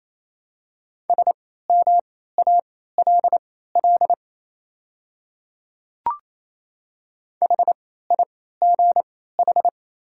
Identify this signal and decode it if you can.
Morse code, heard as a keyed tone, about 28 wpm, spaced as for 16 wpm, 700 Hz. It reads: SMALL HIGH
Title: 3x Character Spacing - Top 500 Words - 28wpm